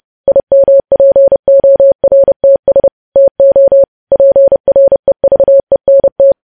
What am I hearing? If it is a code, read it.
Morse code, heard as a keyed tone, about 30 wpm, 550 Hz. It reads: IMPORTS TO PREVENT
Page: news headlines in morse code at 30 wpm